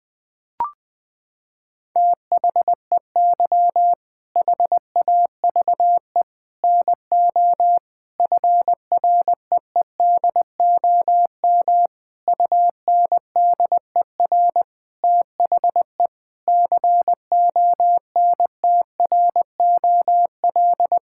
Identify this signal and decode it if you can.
Morse code, heard as a keyed tone, about 20 wpm, 700 Hz. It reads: THEY HAVE NO FREEDOM UNDER THE CONTROL
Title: Sentences from Top 1000 Words - Review Part 5 - 20wpm